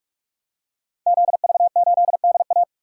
Morse code, heard as a keyed tone, about 44 wpm, 700 Hz. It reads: ZV8DA
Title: Code Groups - 5 Letter:Numbers - 45wpm